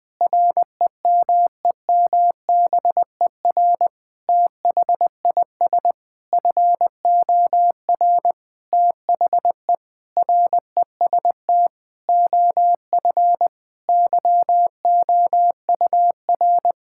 Morse code, heard as a keyed tone, about 21 wpm, 700 Hz. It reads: REMEMBER THIS FOR THE REST OF YOUR
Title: Sentences with 5-Word Phrases - 20wpm